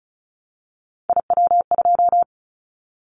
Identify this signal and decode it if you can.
Morse code, heard as a keyed tone, about 36 wpm, 700 Hz. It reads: IW2
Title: Callsign Prefixes for Top 50 DXCC - 35wpm